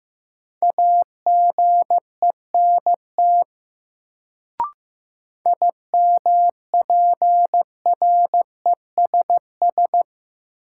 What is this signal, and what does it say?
Morse code, heard as a keyed tone, about 15 wpm, 700 Hz. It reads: AGENT IMPRESS
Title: Sentences from Top 1600 Words - New Word Drill - 15wpm